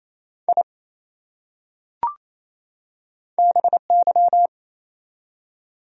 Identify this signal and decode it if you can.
Morse code, heard as a keyed tone, about 28 wpm, 700 Hz. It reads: I BY
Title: Top 100 Words 28wpm